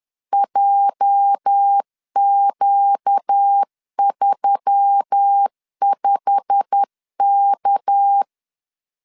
Morse code, heard as a keyed tone, about 10 wpm, 800 Hz. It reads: JQ35K
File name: LEZIONI / 8 YJ45 / 8 YJ45 10wpm 2549